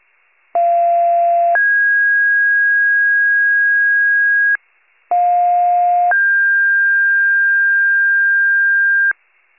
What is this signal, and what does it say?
Two-tone page, heard as a one-second tone, then a three-second tone, about 690 then 1650 Hz, about 710 then 1600 Hz. It is heard from 0.5 to 9.1 s.